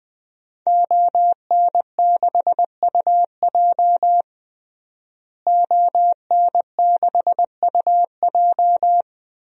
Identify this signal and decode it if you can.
Morse code, heard as a keyed tone, about 20 wpm, 700 Hz. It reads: ON6UJ ON6UJ
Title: DXCC-100 Call Signs - 5 Characters - Repeated 3x - 20wpm